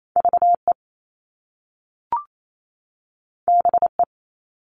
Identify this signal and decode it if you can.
Morse code, heard as a keyed tone, about 28 wpm, 700 Hz. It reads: VE BE
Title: Top Two Letter Combinations in Words 28wpm